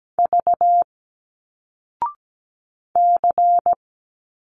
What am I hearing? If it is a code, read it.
Morse code, heard as a keyed tone, about 17 wpm, 700 Hz. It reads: V C
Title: Single Letters 17wpm